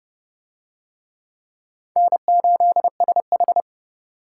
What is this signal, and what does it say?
Morse code, heard as a keyed tone, about 28 wpm, 700 Hz. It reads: N8SH